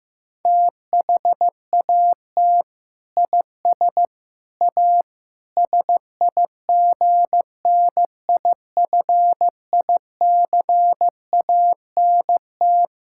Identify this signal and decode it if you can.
Morse code, heard as a keyed tone, about 15 wpm, 700 Hz. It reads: THAT IS A SIGNIFICANT
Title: Sentences from Top 1000 Words - Review Part 6 - 15wpm